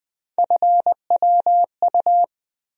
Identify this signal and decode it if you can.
Morse code, heard as a keyed tone, about 20 wpm, 700 Hz. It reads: FWU
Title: USA Call Sign Postfixes - 3 Letters 20wpm